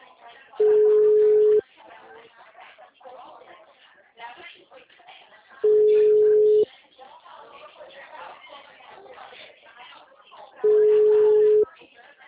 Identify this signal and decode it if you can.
Ringback tone (Europe, 425 Hz), with speech.
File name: out- -2032